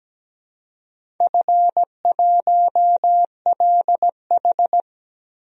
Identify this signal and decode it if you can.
Morse code, heard as a keyed tone, about 17 wpm, 700 Hz. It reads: F1LH
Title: DXCC-100 Call Signs - 4 Characters - 17wpm